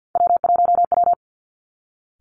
Morse code, heard as a keyed tone, about 50 wpm, 700 Hz. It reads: W1W
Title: Three Challenging Letter-Number Combinations 50wpm